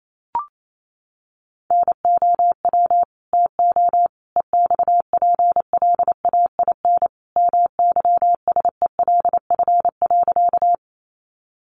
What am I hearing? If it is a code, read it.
Morse code, heard as a keyed tone, about 28 wpm, 700 Hz. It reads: NOW TO EXPLAIN MYSELF.
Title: Sentences from Top 1300 Words 28wpm